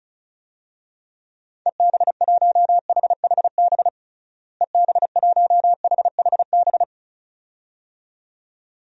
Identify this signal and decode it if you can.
Morse code, heard as a keyed tone, about 34 wpm, 700 Hz. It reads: EB1HHB EB1HHB
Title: DXCC-100 Call Signs - 6 Characters - Repeated 2x - 35wpm